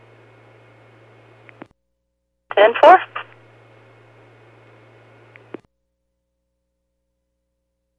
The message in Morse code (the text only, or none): none